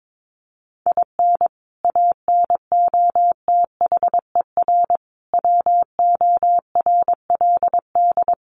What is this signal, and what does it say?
Morse code, heard as a keyed tone, about 22 wpm, 700 Hz. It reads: IN ANOTHER WORLD
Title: Sentences from Top 1100 Words 22wpm